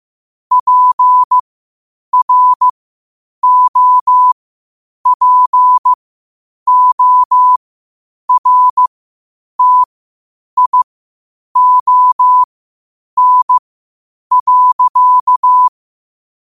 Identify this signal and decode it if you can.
Morse code, heard as a keyed tone, about 15 wpm, spaced as for 8 wpm, 1000 Hz. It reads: PROPORTION.
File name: qotdfiles / 8wpm